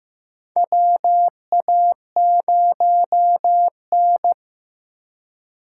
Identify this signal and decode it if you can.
Morse code, heard as a keyed tone, about 15 wpm, 700 Hz. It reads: WA0N
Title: USA Call Signs - 3x Character Spacing - 15wpm